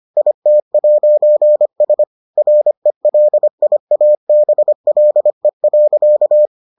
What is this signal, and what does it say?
Morse code, heard as a keyed tone, about 25 wpm, 600 Hz. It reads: IT'S RELIABLE.